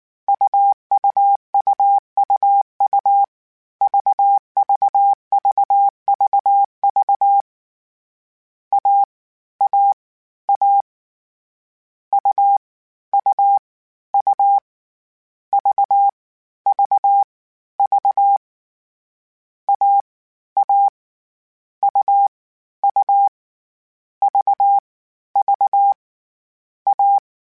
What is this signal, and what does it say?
Morse code, heard as a keyed tone, about 19 wpm, 800 Hz. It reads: UUUUU VVVVV A A A U U U V V V A A U U V V A